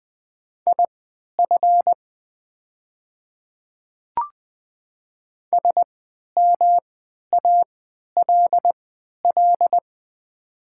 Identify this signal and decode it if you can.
Morse code, heard as a keyed tone, about 20 wpm, spaced as for 11 wpm, 700 Hz. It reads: IF SMALL
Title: 3x Character Spacing - Top 200 Words - 20wpm